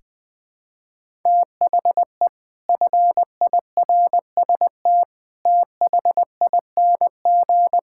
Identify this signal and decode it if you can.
Morse code, heard as a keyed tone, about 20 wpm, 700 Hz. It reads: THE FIRST THING
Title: Sentences with 8-Word Phrases - Spoken Beginning - 20wpm